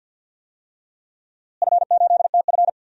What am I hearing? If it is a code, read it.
Morse code, heard as a keyed tone, about 53 wpm, 700 Hz. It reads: F8TF
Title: DXCC-100 Call Signs - 4 Characters - Repeated 3x - 50wpm